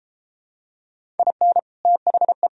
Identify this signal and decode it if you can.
Morse code, heard as a keyed tone, about 33 wpm, 700 Hz. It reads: IN THE